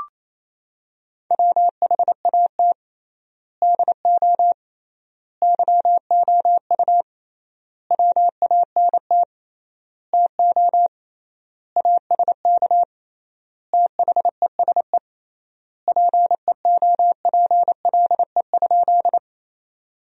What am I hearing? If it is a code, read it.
Morse code, heard as a keyed tone, about 27 wpm, 700 Hz. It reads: WHAT DO YOU WANT TO ASK THESE PEOPLE?
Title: 3x Word-Spacing - Sentences from Top 200 Words - 28wpm